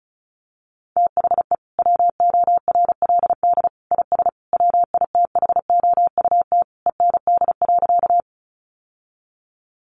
Morse code, heard as a keyed tone, about 34 wpm, 700 Hz. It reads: THE WORLD IS WITHOUT END.
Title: Sentences from Top 600 Words 35wpm